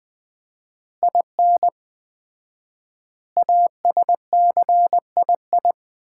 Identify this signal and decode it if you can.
Morse code, heard as a keyed tone, about 20 wpm, 700 Hz. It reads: IN ASCII
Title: 4x Word-Spacing - Sets of 5 Words - 20wpm